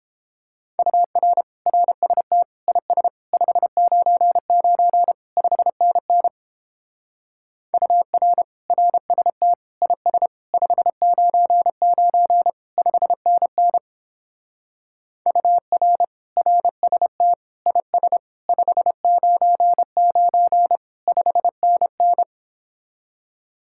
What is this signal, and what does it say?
Morse code, heard as a keyed tone, about 29 wpm, 700 Hz. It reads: UR RST IS 599 5NN UR RST IS 599 5NN UR RST IS 599 5NN